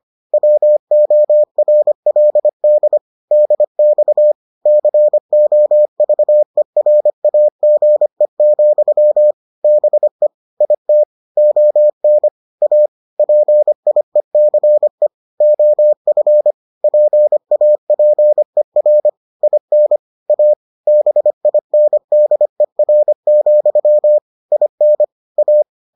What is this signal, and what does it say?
Morse code, heard as a keyed tone, about 25 wpm, 600 Hz. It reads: WORLD DX COVERAGE, BE IT ON A PIECE OF PAPER IN A BINDER, IN A